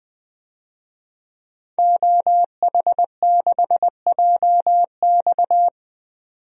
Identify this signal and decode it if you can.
Morse code, heard as a keyed tone, about 20 wpm, 700 Hz. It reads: OH6JX
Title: DXCC-100 Call Signs - 5 Characters - 20wpm